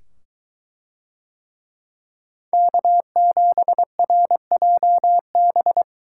Morse code, heard as a keyed tone, about 22 wpm, 700 Hz. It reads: K7RJB